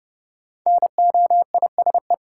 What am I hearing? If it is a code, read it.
Morse code, heard as a keyed tone, about 32 wpm, 700 Hz. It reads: NOISE
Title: Five Letter Words - 30wpm